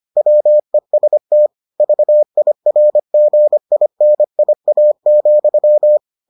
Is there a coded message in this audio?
Yes: Morse code from 0.2 to 6.0 s, heard as a keyed tone, about 25 wpm, 600 Hz. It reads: WEST VIRGINIA,